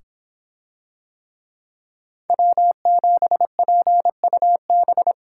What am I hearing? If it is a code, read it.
Morse code, heard as a keyed tone, about 25 wpm, 700 Hz. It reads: W7PUB